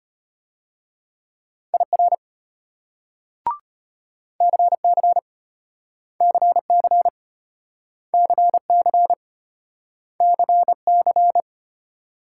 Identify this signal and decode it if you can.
Morse code, heard as a keyed tone, about 34 wpm, 700 Hz. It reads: IR CC CC CC CC